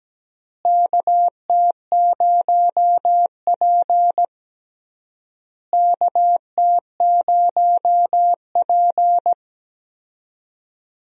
Morse code, heard as a keyed tone, about 17 wpm, 700 Hz. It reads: KT0P KT0P
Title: USA Call Signs - 4 Characters - Repeated 3x - 17wpm